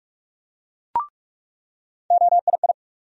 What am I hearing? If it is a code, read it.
Morse code, heard as a keyed tone, about 45 wpm, 700 Hz. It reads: OII
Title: USA Call Sign Postfixes - 3 Letters 45wpm